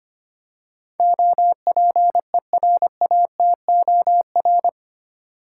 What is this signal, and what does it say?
Morse code, heard as a keyed tone, about 25 wpm, 700 Hz. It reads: OPERATOR
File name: Eight Letter Words - 25wpm